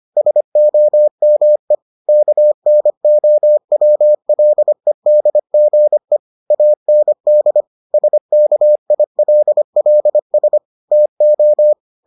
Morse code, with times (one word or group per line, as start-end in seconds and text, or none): 0.2-1.7 SOME
2.1-6.2 KNOWLEDGE
6.5-7.6 AND
7.9-10.6 SKILLS
10.9-11.7 TO